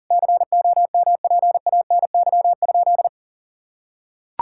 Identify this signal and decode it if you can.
Morse code, heard as a keyed tone, about 40 wpm, 700 Hz. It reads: COMPANY?
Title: Sentences from Top 1600 Words 40wpm